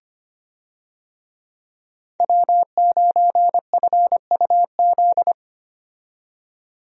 Morse code, heard as a keyed tone, about 25 wpm, 690 Hz. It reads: W9FUZ
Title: USA Call Signs - 5 Characters - 25wpm